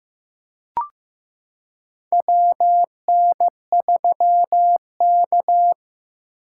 Morse code, heard as a keyed tone, about 15 wpm, 700 Hz. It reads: WN3K
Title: K1USN SST QSO Practice - 15wpm